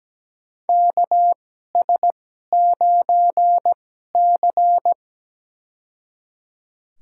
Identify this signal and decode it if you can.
Morse code, heard as a keyed tone, about 17 wpm, spaced as for 12 wpm, 700 Hz. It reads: KS9C
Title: USA Call Signs - 4 Characters - 2x Character Spacing - 17wpm